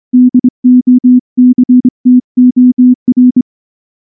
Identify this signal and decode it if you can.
Morse code, heard as a keyed tone, about 25 wpm, 250 Hz. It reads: DOCTOR
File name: morse / morse-training-25WPM